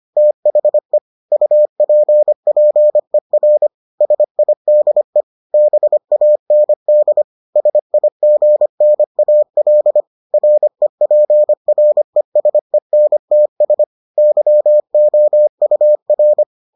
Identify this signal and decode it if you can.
Morse code, heard as a keyed tone, about 25 wpm, 600 Hz. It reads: THE UPPER SIDE BAND SIGNAL REPRESENTS YOUR